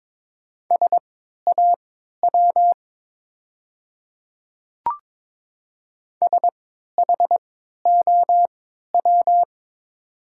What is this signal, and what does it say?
Morse code, heard as a keyed tone, about 22 wpm, spaced as for 13 wpm, 700 Hz. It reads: SAW SHOW
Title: 3x Character Spacing - Top 300 Words - 22wpm